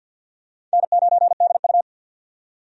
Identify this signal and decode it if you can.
Morse code, heard as a keyed tone, about 52 wpm, 700 Hz. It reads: N9DU
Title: USA Call Signs - 4 Characters - 50wpm